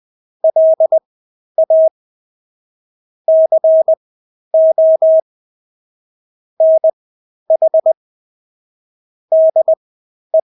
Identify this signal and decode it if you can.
Morse code, heard as a keyed tone, about 20 wpm, spaced as for 11 wpm, 640 Hz. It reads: LA CO NH DE
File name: State-Abbreviations-10